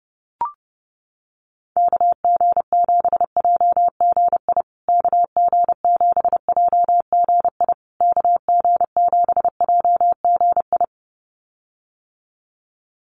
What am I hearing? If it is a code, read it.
Morse code, heard as a keyed tone, about 30 wpm, 700 Hz. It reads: KG7JGI KG7JGI KG7JGI
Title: USA Call Signs Repeated 3x 30wpm